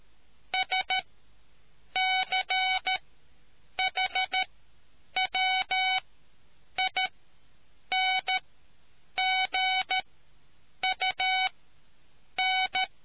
Morse code, heard as a keyed tone, about 13 wpm, spaced as for 7 wpm, 750 Hz. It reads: SCHWINGUN